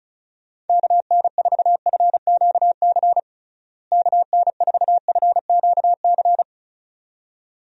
Morse code, heard as a keyed tone, about 34 wpm, 700 Hz. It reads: KN4FQC KN4FQC
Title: USA Call Signs - 6 Characters - Repeated 2x - 35wpm